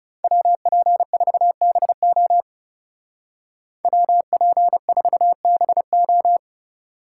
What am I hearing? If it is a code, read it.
Morse code, heard as a keyed tone, about 39 wpm, 700 Hz. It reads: WP4BO WP4BO